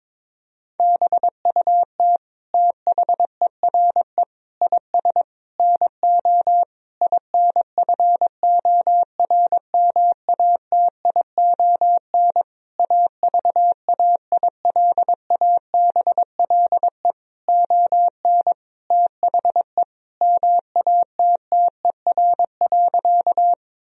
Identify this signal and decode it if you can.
Morse code, heard as a keyed tone, about 22 wpm, 700 Hz. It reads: BUT THERE IS NO INFORMATION AVAILABLE ON THE MATTER.
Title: Sentences from Top 1000 Words - Review Part 3 - 22wpm